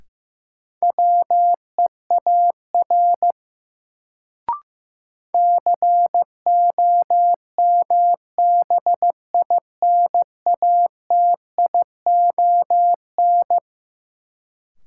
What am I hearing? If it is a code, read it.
Morse code, heard as a keyed tone, about 15 wpm, 700 Hz. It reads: WEAR COMBINATION
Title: Sentences from Top 1000 Words - Review Part 6 - New Word Drill - 15wpm